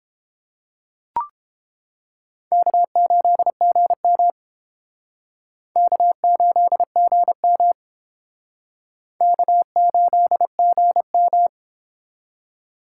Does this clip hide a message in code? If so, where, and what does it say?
Yes: Morse code from 2.5 to 11.5 s, heard as a keyed tone, about 30 wpm, 700 Hz. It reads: K8GM K8GM K8GM